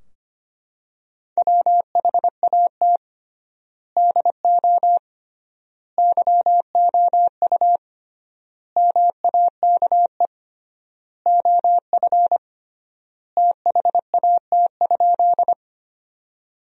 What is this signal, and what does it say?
Morse code, heard as a keyed tone, about 25 wpm, 700 Hz. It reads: WHAT DO YOU MAKE OF THAT?
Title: 3x Word-Spacing - Sentences from Top 100 Words - 25wpm